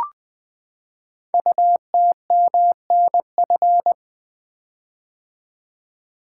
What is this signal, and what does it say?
Morse code, heard as a keyed tone, about 20 wpm, 700 Hz. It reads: UTMNF
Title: Code Groups - 5 Letters - 20wpm